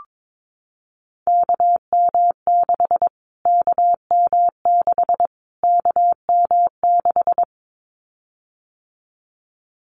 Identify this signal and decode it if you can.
Morse code, heard as a keyed tone, about 22 wpm, 700 Hz. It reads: KM6 KM6 KM6
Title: Callsign Prefixes for Top 25 DXCC - Repeated 3x - 22wpm